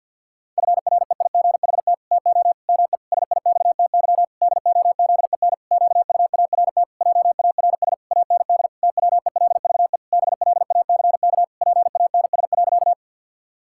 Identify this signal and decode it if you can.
Morse code, heard as a keyed tone, about 50 wpm, 700 Hz. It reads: FREIGHT TO ME SIXTY DOZEN QUART JARS AND TWELVE BLACK PANS.
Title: Pangram Challenge v2 - 50wpm